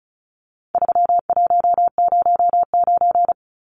Morse code, heard as a keyed tone, about 34 wpm, 700 Hz. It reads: 3109
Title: Numbers - 4-Digit 35wpm